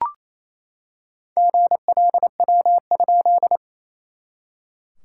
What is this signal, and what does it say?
Morse code, heard as a keyed tone, about 27 wpm, 660 Hz. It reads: GLW?
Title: SOTA QSO Practice 28wpm